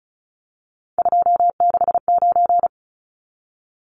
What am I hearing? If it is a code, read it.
Morse code, heard as a keyed tone, about 34 wpm, 700 Hz. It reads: 269
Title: Numbers - 3-Digit 35wpm